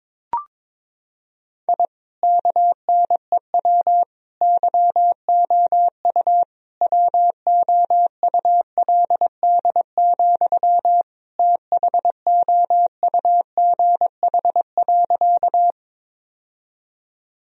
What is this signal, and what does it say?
Morse code, heard as a keyed tone, about 22 wpm, 700 Hz. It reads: I KNEW YOU WOULD, THOUGH.
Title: Sentences from Top 1000 Words - Review Part 2 - 22wpm